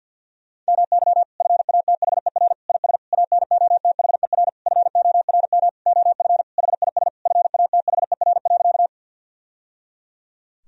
Morse code, heard as a keyed tone, about 51 wpm, 700 Hz. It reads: MY FATHER IS ANOTHER FORM OF HIS FATHER.
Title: Sentences from Top 1000 Words - Review Part 1 - 50wpm